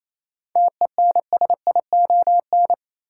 Morse code, heard as a keyed tone, about 28 wpm, 700 Hz. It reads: TENSION
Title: Sentences from Top 1600 Words - New Word Drill - 28wpm